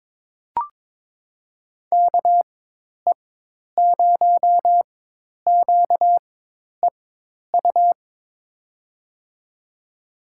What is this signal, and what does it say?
Morse code, heard as a keyed tone, about 21 wpm, spaced as for 10 wpm, 700 Hz. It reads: KE0QEU
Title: USA Call Signs - 6 Characters - 4x Character Spacing - 22wpm